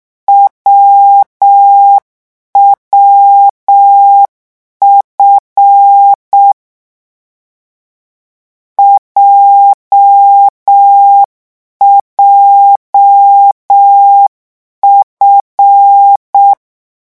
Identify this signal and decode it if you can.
Morse code, heard as a keyed tone, about 6 wpm, 800 Hz. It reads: WWF JJF